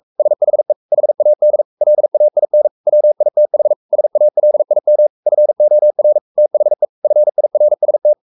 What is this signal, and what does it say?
Morse code, heard as a keyed tone, about 44 wpm, 600 Hz. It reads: SHE HAD LAIN WITH SALIM FOR THE FIRST
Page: Romanen Never af Ken Follet på engelsk.